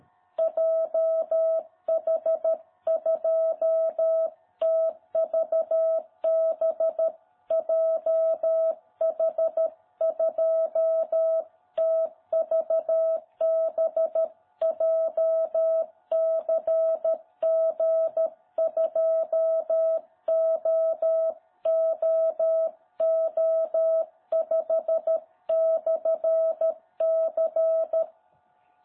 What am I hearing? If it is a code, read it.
Morse code, heard as a keyed tone, about 13 wpm, 660 Hz. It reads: JH2TVB JH2TVB JCG2OOO5/C
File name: jh8cbh / 2021cw